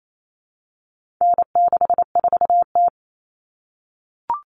Morse code, heard as a keyed tone, about 28 wpm, 700 Hz. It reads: N64T
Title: Common Cut Numbers - 4-Digit 28wpm